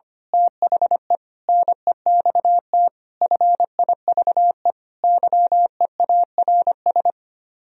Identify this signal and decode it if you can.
Morse code, heard as a keyed tone, about 24 wpm, 700 Hz. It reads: THE NEXT FIVE YEARS
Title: Speed-Racing - Sets of 5 Words 25wpm